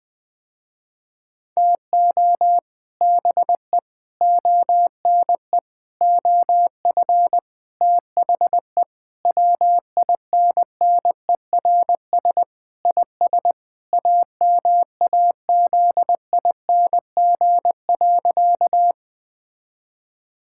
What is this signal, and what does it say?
Morse code, heard as a keyed tone, about 19 wpm, 700 Hz. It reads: TO BE ONE OF THE WINNERS IS AMAZING.
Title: Sentences with 5-Word Phrases - Spoken Beginning - 20wpm